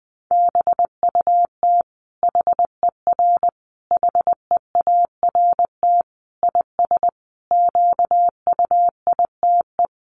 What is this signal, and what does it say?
Morse code, heard as a keyed tone, about 20 wpm, 700 Hz. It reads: BUT HER HEART IS QUITE
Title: Sentences from Top 1200 Words 20wpm